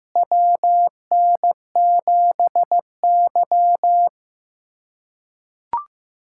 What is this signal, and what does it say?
Morse code, heard as a keyed tone, about 16 wpm, 700 Hz. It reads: WN7Y
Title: USA Callsign DE Callsign - 4 Characters - 15wpm